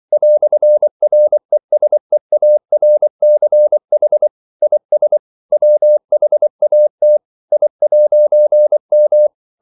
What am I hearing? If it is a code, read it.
Morse code, heard as a keyed tone, about 24 wpm, 600 Hz. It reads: "RESEARCH IS WHAT I'M
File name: CW / Practice / EnglishQuotations0424